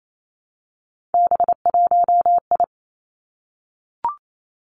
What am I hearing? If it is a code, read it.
Morse code, heard as a keyed tone, about 28 wpm, 700 Hz. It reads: B1I
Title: Three Challenging Letter-Number Combinations 28wpm